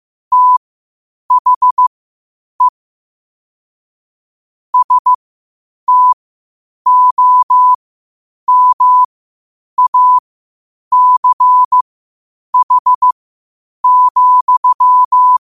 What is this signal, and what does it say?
Morse code, heard as a keyed tone, about 15 wpm, spaced as for 8 wpm, 1000 Hz. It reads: THE STOMACH,
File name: qotdfiles / 8wpm